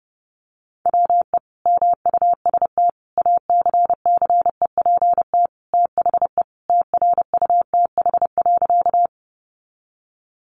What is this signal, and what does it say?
Morse code, heard as a keyed tone, about 30 wpm, 700 Hz. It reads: WE MUST ACCEPT THE TRUTH.
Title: Sentences from Top 1300 Words 30wpm